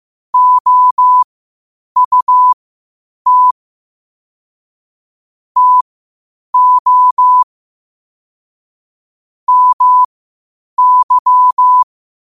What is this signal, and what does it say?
Morse code, heard as a keyed tone, about 15 wpm, spaced as for 8 wpm, 1000 Hz. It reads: OUT TO MY